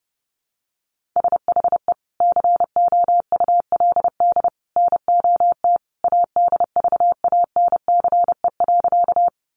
Morse code, heard as a keyed tone, about 30 wpm, 700 Hz. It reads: SHE COULD NOT ADVANCE.
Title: Sentences from Top 1400 Words 30wpm